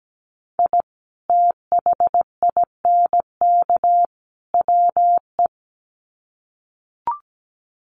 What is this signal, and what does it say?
Morse code, heard as a keyed tone, about 17 wpm, 700 Hz. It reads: I THINK WE
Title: Sets of 3 Words 17wpm